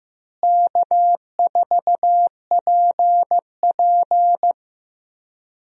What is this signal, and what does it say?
Morse code, heard as a keyed tone, about 15 wpm, 700 Hz. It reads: K4PP
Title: USA Call Signs - 4 Characters - Repeated 3x - 15wpm